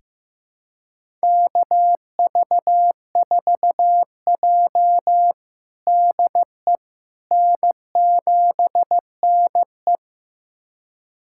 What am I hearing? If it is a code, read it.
Morse code, heard as a keyed tone, about 15 wpm, 700 Hz. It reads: KV4J DE N7NE
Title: USA Callsign DE Callsign - 4 Characters - 15wpm